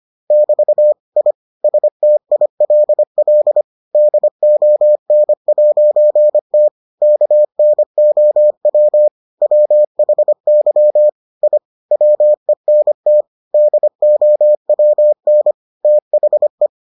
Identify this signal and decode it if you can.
Morse code, heard as a keyed tone, about 25 wpm, 600 Hz. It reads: <BT> I STILL DON'T KNOW WHY I WENT DOWN THE